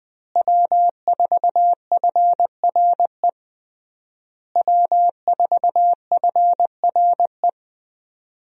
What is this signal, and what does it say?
Morse code, heard as a keyed tone, about 20 wpm, 700 Hz. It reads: W4FRE W4FRE
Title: USA Call Signs - 5 Characters - Repeated 3x - 20wpm